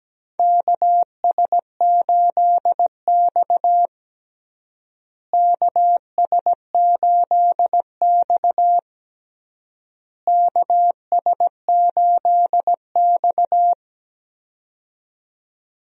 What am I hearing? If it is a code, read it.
Morse code, heard as a keyed tone, about 17 wpm, 700 Hz. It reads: KS8X KS8X KS8X
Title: USA Call Signs - 4 Characters - Repeated 3x - 17wpm